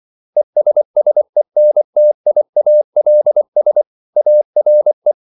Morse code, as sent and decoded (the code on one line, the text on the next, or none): . ... ... -. - .. .- .-.. ... / .- .-. .
ESSENTIALS ARE